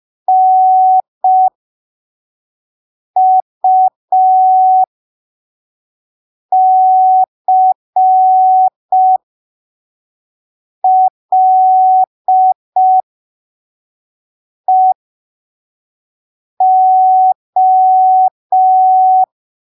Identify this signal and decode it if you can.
Morse code, heard as a keyed tone, about 5 wpm, spaced as for 3 wpm, 750 Hz. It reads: NUCLEO
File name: arquivos / texto / texto 11 05ppm